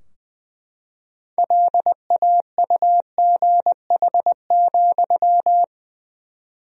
Morse code, heard as a keyed tone, about 21 wpm, 700 Hz. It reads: LAUGH,
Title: 4x Word-Spacing - Sentences from Top 500 Words - 20wpm